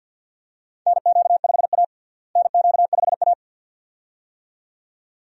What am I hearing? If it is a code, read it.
Morse code, heard as a keyed tone, about 47 wpm, 700 Hz. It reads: NQ5A NQ5A
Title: USA Call Signs - 4 Characters - Repeated 2x - 50wpm